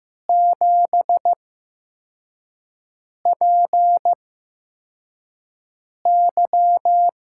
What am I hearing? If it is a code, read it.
Morse code, heard as a keyed tone, about 15 wpm, spaced as for 4 wpm, 700 Hz. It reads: 7PY
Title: USA Call Signs - 8x Character Spacing - 15wpm